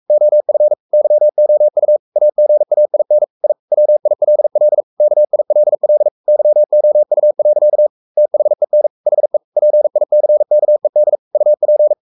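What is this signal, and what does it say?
Morse code, heard as a keyed tone, about 44 wpm, 600 Hz. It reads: OF YOU AGAIN I WILL KILL YOU. THEN HE PICKED UP
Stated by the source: Morse code at 43 wpm sending Ken Follett's novel Never in English, from CW med Gnister.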